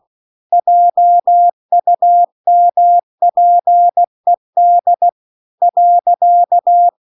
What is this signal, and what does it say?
Morse code, heard as a keyed tone, about 16 wpm, 700 Hz. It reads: JUMPED .
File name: CW-The-Big-Twoheaded-River-2-16WPM